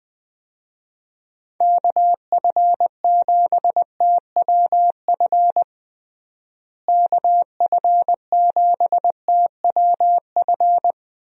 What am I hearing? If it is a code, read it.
Morse code, heard as a keyed tone, about 20 wpm, 700 Hz. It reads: KF7TWF KF7TWF
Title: USA Call Signs - 6 Characters - Repeated 3x - 20wpm